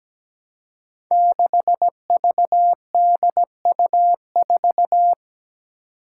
Morse code, heard as a keyed tone, about 17 wpm, 700 Hz. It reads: 6VDU4
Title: Five Challenging Letter-Number Combinations 17wpm